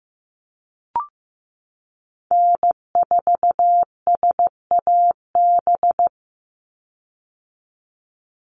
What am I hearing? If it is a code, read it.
Morse code, heard as a keyed tone, about 15 wpm, 700 Hz. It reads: N4SAB
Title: USA Call Signs 15wpm